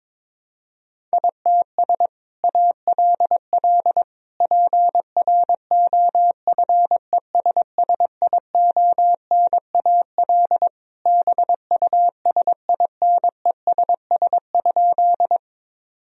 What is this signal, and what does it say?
Morse code, heard as a keyed tone, about 22 wpm, 700 Hz. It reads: ITS ALL PROFESSIONAL BUSINESS?
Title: Sentences from Top 1000 Words - Review Part 4 - 22wpm